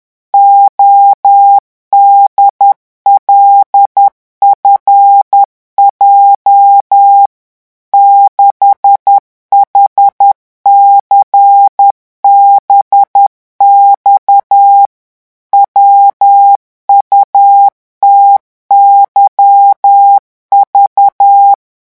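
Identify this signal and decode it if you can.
Morse code, heard as a keyed tone, about 10 wpm, 800 Hz. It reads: ODLFJ 6HCBX WUTYV
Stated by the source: Morse revision exercise for French radio amateurs at 10 words per minute.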